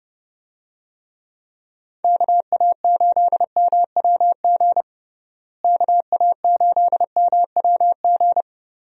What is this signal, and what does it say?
Morse code, heard as a keyed tone, about 32 wpm, 700 Hz. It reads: KA8MWG KA8MWG
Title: USA Call Signs - 6 Characters - Repeated 2x - 30wpm